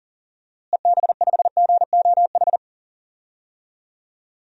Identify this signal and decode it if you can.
Morse code, heard as a keyed tone, about 37 wpm, 700 Hz. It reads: EB5GOH
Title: DXCC-100 Call Signs - 6 Characters - Repeated 2x - 40wpm